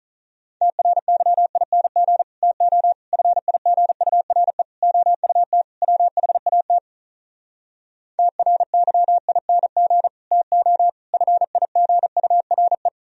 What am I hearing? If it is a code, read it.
Morse code, heard as a keyed tone, about 47 wpm, 700 Hz. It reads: TRYING TO FIGURE OUT WHAT TRYING TO FIGURE